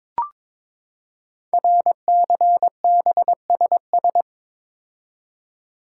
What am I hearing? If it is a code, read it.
Morse code, heard as a keyed tone, about 22 wpm, 700 Hz. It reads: RCBSS
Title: Code Groups - 5 Letters - 22wpm